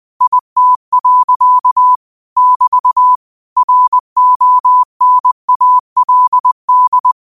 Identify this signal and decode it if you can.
Morse code, heard as a keyed tone, about 20 wpm, 1000 Hz. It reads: IT. <BT> RONALD